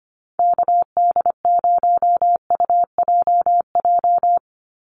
Morse code, heard as a keyed tone, about 25 wpm, 700 Hz. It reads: KD0UJJ
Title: USA Call Signs 25wpm